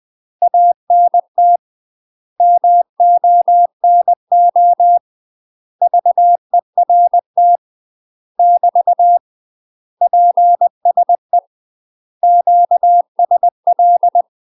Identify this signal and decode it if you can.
Morse code, heard as a keyed tone, about 20 wpm, 700 Hz. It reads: ANT MONO VERT <BT> PSE QSL
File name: UK MASTER QSOs TRAINING/20WPM